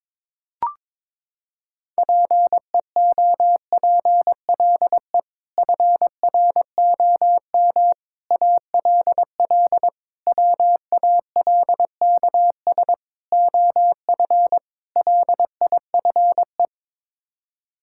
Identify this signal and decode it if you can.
Morse code, heard as a keyed tone, about 22 wpm, 700 Hz. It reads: PEOPLE FROM ALL WALKS OF LIFE
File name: practice / 22wpm / Sets of 6 Words - 22wpm